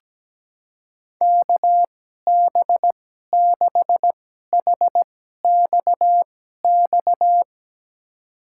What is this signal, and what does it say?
Morse code, heard as a keyed tone, about 17 wpm, spaced as for 12 wpm, 700 Hz. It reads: KB6HXX
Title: USA Call Signs - 6 Characters - 2x Character Spacing - 17wpm